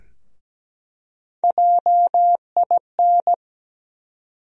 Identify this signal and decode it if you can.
Morse code, heard as a keyed tone, about 17 wpm, 700 Hz. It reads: JIN
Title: USA Call Sign Postfixes - 3 Letters 17wpm